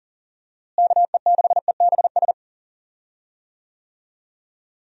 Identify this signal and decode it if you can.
Morse code, heard as a keyed tone, about 40 wpm, 700 Hz. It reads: KE6EBS
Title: USA Call Signs - 6 Characters - Repeated 3x - 40wpm